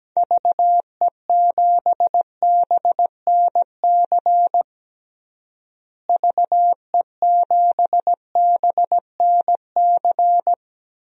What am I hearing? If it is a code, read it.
Morse code, heard as a keyed tone, about 17 wpm, 700 Hz. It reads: VE7BNC VE7BNC
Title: DXCC-100 Call Signs - 6 Characters - Repeated 2x - 17wpm